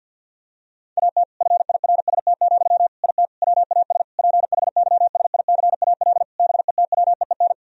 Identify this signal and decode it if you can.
Morse code, heard as a keyed tone, about 50 wpm, 700 Hz. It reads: AT FIRST, IT WAS PHYSICAL BETWEEN
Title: Sentences from Top 1000 Words - Review Part 4 - 50wpm